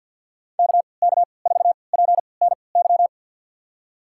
Morse code, heard as a keyed tone, about 52 wpm, spaced as for 29 wpm, 700 Hz. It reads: KK4PNY
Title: USA Call Signs - 6 Characters - 3x Character Spacing - 50wpm